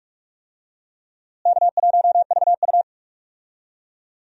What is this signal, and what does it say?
Morse code, heard as a keyed tone, about 43 wpm, 700 Hz. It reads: K1VU
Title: USA Call Signs - 4 Characters - Repeated 3x - 45wpm